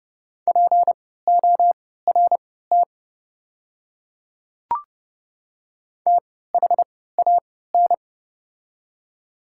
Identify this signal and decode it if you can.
Morse code, heard as a keyed tone, about 30 wpm, spaced as for 17 wpm, 700 Hz. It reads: PORT THAN
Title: 3x Character Spacing - Top 300 Words - 30wpm